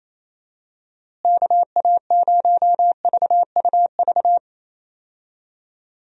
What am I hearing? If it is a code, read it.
Morse code, heard as a keyed tone, about 27 wpm, 700 Hz. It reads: KA0VUV
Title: USA Call Signs - 6 Characters - Repeated 2x - 28wpm